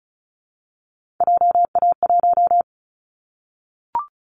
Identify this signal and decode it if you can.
Morse code, heard as a keyed tone, about 35 wpm, 700 Hz. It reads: JA1
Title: Callsign Prefixes for Top 50 DXCC - 35wpm